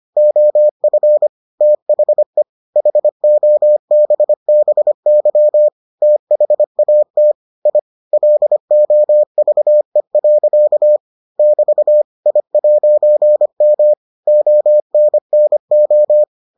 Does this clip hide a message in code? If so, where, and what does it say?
Yes: Morse code from 0.2 to 16.2 s, heard as a keyed tone, about 25 wpm, 600 Hz. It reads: OF THE HOBBY THAT I LOVE. <BT> I'M ONNO